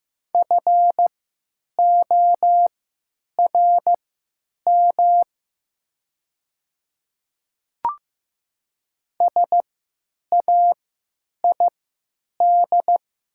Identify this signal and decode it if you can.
Morse code, heard as a keyed tone, about 15 wpm, spaced as for 9 wpm, 700 Hz. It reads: FORM SAID
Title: 3x Character Spacing - Top 200 Words - 15wpm